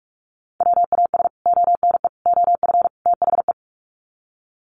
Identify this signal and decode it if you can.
Morse code, heard as a keyed tone, about 44 wpm, 700 Hz. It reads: WAS ONE OF THE
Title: Sets of 4 Words 45wpm